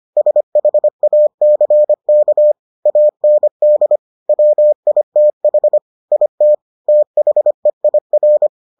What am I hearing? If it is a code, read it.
Morse code, heard as a keyed tone, about 25 wpm, 600 Hz. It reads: SHACK AND WITH IT THEIR